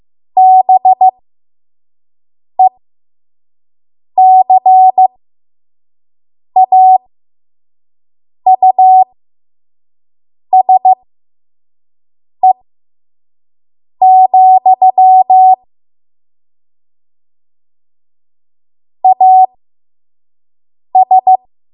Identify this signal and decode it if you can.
Morse code, heard as a keyed tone, about 15 wpm, spaced as for 5 wpm, 750 Hz. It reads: BECAUSE, AS
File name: Archive / 5 WPM / 05WPM